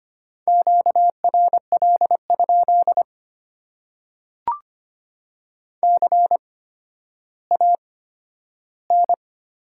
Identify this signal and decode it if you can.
Morse code, heard as a keyed tone, about 25 wpm, spaced as for 12 wpm, 700 Hz. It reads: QRL? CAN
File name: Top 90 QSO Elements - 8x Character Spacing - 25wpm